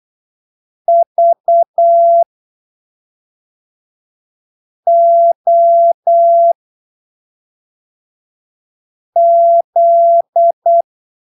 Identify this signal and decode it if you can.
Morse code, heard as a keyed tone, about 8 wpm, spaced as for 3 wpm, 680 Hz. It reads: VOZ